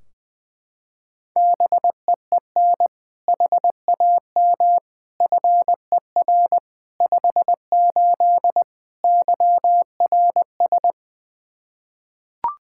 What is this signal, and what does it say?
Morse code, heard as a keyed tone, about 20 wpm, 700 Hz. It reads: BEEN HAM FER 58 YRS
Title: Practice QSO - Exchange 3 - More Info - 20wpm